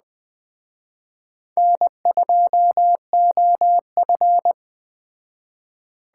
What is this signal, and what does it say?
Morse code, heard as a keyed tone, about 20 wpm, 700 Hz. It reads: N2OF
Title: Speed Racing - USA 1x2 Call Signs 17wpm